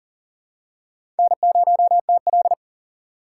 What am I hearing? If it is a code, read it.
Morse code, heard as a keyed tone, about 40 wpm, 700 Hz. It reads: N0TL
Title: USA Call Signs - 4 Characters - 2x Character Spacing - 40wpm